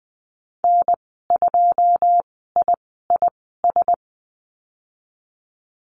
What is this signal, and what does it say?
Morse code, heard as a keyed tone, about 20 wpm, spaced as for 15 wpm, 700 Hz. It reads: N2IIS
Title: 088 Lesson 15g - Number 5 - Call Signs 20wpm